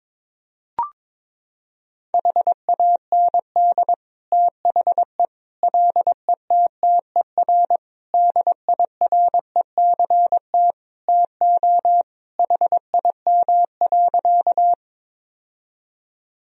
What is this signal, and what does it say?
Morse code, heard as a keyed tone, about 21 wpm, 700 Hz. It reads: HAND THE LETTER DIRECT TO HIM.
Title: Sentences from Top 1000 Words - Review Part 2 - 22wpm